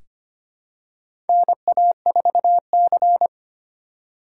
Morse code, heard as a keyed tone, about 25 wpm, 700 Hz. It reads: NA4C
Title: K1USN SST QSO Practice - 25wpm